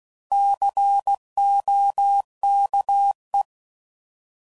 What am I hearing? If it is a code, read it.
Morse code, heard as a keyed tone, about 16 wpm, 800 Hz. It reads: COKE
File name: Prefixes / By First Letter / Cs